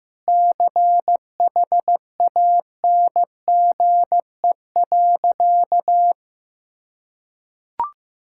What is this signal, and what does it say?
Morse code, heard as a keyed tone, about 15 wpm, 700 Hz. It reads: CHANGE.
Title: Sentences from Top 1000 Words - Review Part 6 - 15wpm